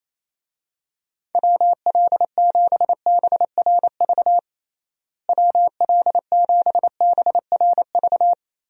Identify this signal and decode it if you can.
Morse code, heard as a keyed tone, about 28 wpm, 700 Hz. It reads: WL7BRV WL7BRV
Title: USA Call Signs - 6 Characters - Repeated 2x - 28wpm